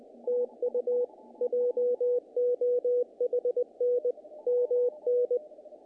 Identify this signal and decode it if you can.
Morse code, heard as a keyed tone, about 20 wpm, 490 Hz. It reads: TU JOHN MN
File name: contests / naqp-cw-k5tr-op-k5pi- -1800